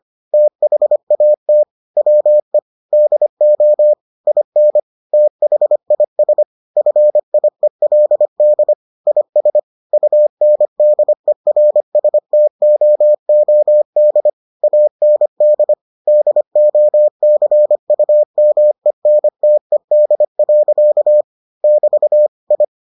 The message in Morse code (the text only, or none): THAT WE DO IN THIS FIELD IS UNDERSTOOD AND DOCUMENTED. <BT> I